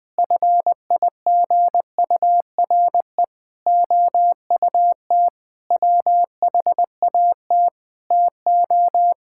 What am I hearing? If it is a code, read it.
Morse code, heard as a keyed tone, about 20 wpm, 700 Hz. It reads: FIGURE OUT WHAT TO
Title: Sentences with 6-Word Phrases - Spoken Beginning - 20wpm